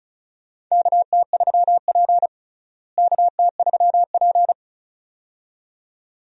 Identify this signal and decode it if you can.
Morse code, heard as a keyed tone, about 36 wpm, 700 Hz. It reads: KT3P KT3P
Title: USA Call Signs - 4 Characters - Repeated 2x - 35wpm